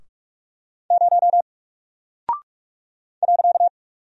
Morse code, heard as a keyed tone, about 44 wpm, 700 Hz. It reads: 0 .
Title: Single Character (Letter:Number:Punctuation:Prosign) Alt 45wpm